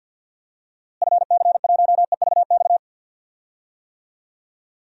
Morse code, heard as a keyed tone, about 51 wpm, 700 Hz. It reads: FC1EVX
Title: DXCC-100 Call Signs - 6 Characters - Repeated 3x - 50wpm